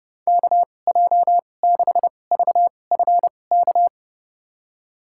Morse code, heard as a keyed tone, about 30 wpm, spaced as for 22 wpm, 700 Hz. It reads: KJ6VFK
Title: USA Call Signs - 6 Characters - 2x Character Spacing - 30wpm